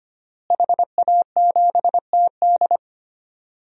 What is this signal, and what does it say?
Morse code, heard as a keyed tone, about 25 wpm, 700 Hz. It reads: HA7TD
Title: DXCC-100 Call Signs - 5 Characters - Repeated 3x - 25wpm